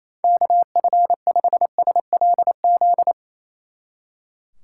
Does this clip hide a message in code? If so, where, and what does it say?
Yes: Morse code from 0.2 to 3.1 s, heard as a keyed tone, about 35 wpm, 700 Hz. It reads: KF5SLZ